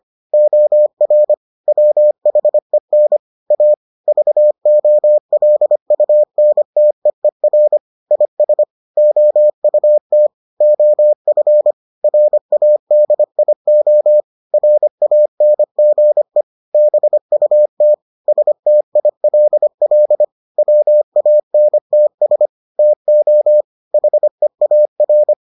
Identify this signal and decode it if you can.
Morse code, heard as a keyed tone, about 25 wpm, 600 Hz. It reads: OR WHEN A VOLUNTEER IS OUT OF RADIO RANGE BUT STILL WANTS TO HEAR